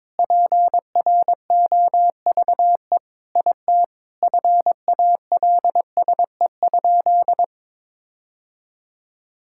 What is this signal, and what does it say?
Morse code, heard as a keyed tone, about 21 wpm, 700 Hz. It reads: PROVE IT FALSE?
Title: Sentences from Top 1600 Words 22wpm